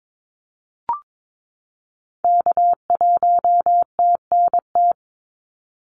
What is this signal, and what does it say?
Morse code, heard as a keyed tone, about 22 wpm, 700 Hz. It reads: K1TNT
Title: NA SOTA Activator Callsigns 22wpm